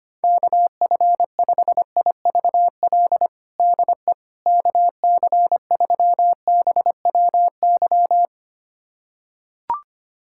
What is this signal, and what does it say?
Morse code, heard as a keyed tone, about 25 wpm, 700 Hz. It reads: KF5IVL DE KC3BWY
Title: USA Callsign DE Callsign - 6 Characters - 25wpm